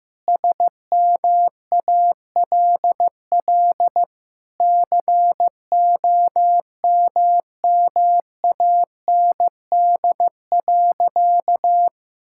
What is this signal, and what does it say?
Morse code, heard as a keyed tone, about 15 wpm, 700 Hz. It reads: SMALL COMMAND.
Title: Sentences from Top 1600 Words 15wpm